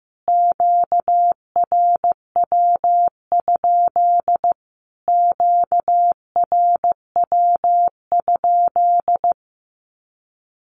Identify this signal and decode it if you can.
Morse code, heard as a keyed tone, about 14 wpm, 700 Hz. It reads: QRW? QRW?
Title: Q-Codes Repeated 3x 15wpm